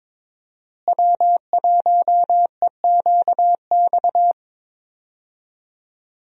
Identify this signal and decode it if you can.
Morse code, heard as a keyed tone, about 22 wpm, 700 Hz. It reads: W1EQX
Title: USA Call Signs - 5 Characters - Repeated 3x - 22wpm